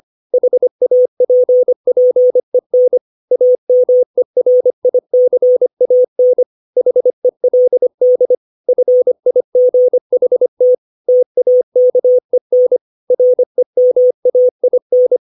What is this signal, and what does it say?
Morse code, heard as a keyed tone, about 25 wpm, 500 Hz. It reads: HAPPEN AMERICAN HELD FIGHT TAKEN REMAIN
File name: 632 words / 632 Words 25WPM